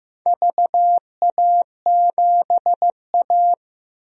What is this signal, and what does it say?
Morse code, heard as a keyed tone, about 15 wpm, 700 Hz. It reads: VA7A
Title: DXCC-100 Call Signs - 4 Characters - 15wpm